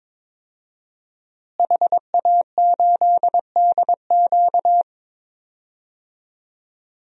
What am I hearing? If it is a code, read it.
Morse code, heard as a keyed tone, about 22 wpm, 700 Hz. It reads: HA8DQ
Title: DXCC-100 Call Signs - 5 Characters - Repeated 3x - 22wpm